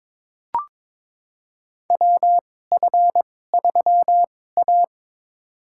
Morse code, heard as a keyed tone, about 23 wpm, spaced as for 16 wpm, 700 Hz. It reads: WF3A
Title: USA Call Signs - 4 Characters - 2x Character Spacing - 22wpm